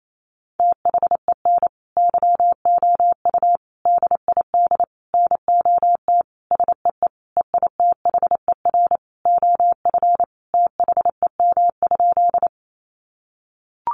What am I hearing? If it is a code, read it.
Morse code, heard as a keyed tone, about 28 wpm, 700 Hz. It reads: THEN YOU DID NOT SEE EITHER OF THEM?
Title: Sentences from Top 600 Words 28wpm